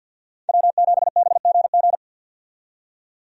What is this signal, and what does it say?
Morse code, heard as a keyed tone, about 47 wpm, 700 Hz. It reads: W7BGG
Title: USA Call Signs - 5 Characters - 50wpm